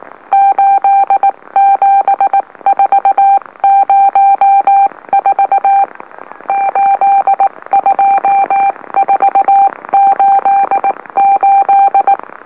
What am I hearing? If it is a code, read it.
Morse code, heard as a keyed tone, about 18 wpm, 790 Hz. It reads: 87404 82488
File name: M04- -2345utc-LOLO